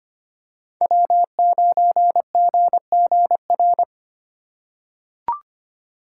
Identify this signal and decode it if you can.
Morse code, heard as a keyed tone, about 25 wpm, 700 Hz. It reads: W9GGR
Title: USA Call Signs - 2x Character Spacing - 25wpm